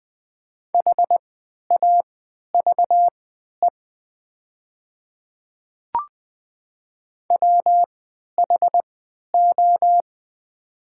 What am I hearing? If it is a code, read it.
Morse code, heard as a keyed tone, about 20 wpm, spaced as for 11 wpm, 700 Hz. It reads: HAVE WHO
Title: 3x Character Spacing - Top 100 Words - 20wpm